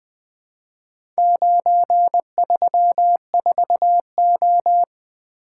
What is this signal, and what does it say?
Morse code, heard as a keyed tone, about 20 wpm, 700 Hz. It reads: 934O
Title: Four Letter-Number Combinations - 20wpm